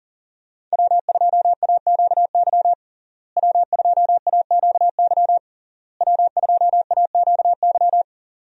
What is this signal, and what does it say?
Morse code, heard as a keyed tone, about 40 wpm, 700 Hz. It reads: W2AQY W2AQY W2AQY
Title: USA Call Signs - 5 Characters - Repeated 3x - 40wpm